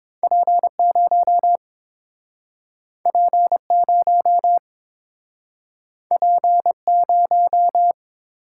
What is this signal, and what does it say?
Morse code, heard as a keyed tone, about 30 wpm, 700 Hz. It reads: P0 P0 P0